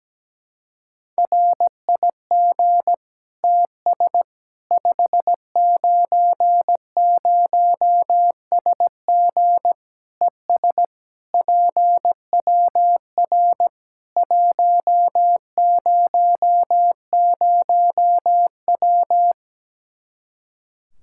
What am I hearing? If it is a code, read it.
Morse code, heard as a keyed tone, about 17 wpm, 700 Hz. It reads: RIG TS 590SG ES PWR 100W
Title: Practice QSO - Exchange 2 - Info - 17wpm